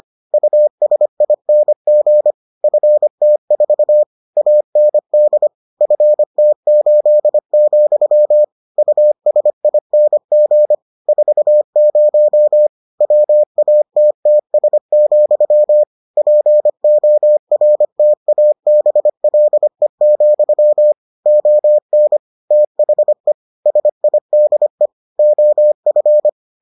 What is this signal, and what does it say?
Morse code, heard as a keyed tone, about 25 wpm, 600 Hz. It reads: USING FT4 AND FT8, USING 40 WATTS, PORTABLE, ON THE SIDE OF